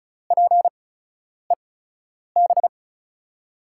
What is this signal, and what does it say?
Morse code, heard as a keyed tone, about 36 wpm, spaced as for 10 wpm, 700 Hz. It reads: PEB
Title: USA Call Signs - 8x Character Spacing - 35wpm